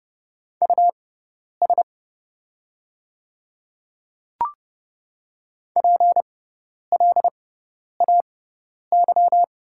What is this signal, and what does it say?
Morse code, heard as a keyed tone, about 30 wpm, spaced as for 10 wpm, 700 Hz. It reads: US PLAY
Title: 6x Character Spacing - Top 200 Words - 30wpm